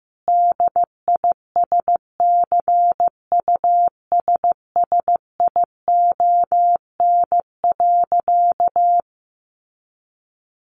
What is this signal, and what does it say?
Morse code, heard as a keyed tone, about 15 wpm, 700 Hz. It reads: DISCUSSION.
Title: Sentences from Top 1100 Words 15wpm